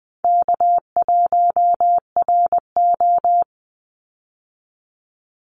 Morse code, heard as a keyed tone, about 20 wpm, 700 Hz. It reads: K1RO
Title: Easy Call Signs - 20wpm